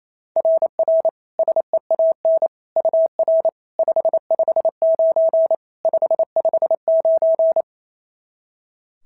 Morse code, heard as a keyed tone, about 28 wpm, 660 Hz. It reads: RR SEAN UR 559 559
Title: SOTA QSO Practice 28wpm